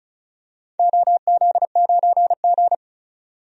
Morse code, heard as a keyed tone, about 34 wpm, 700 Hz. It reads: OZ9G
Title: DXCC-100 Call Signs - 4 Characters - 35wpm